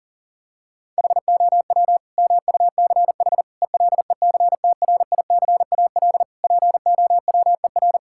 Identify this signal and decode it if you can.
Morse code, heard as a keyed tone, about 40 wpm, 700 Hz. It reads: HOW MUCH ELECTRICAL POWER
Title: Sentences from Top 1000 Words - Review Part 6 - 40wpm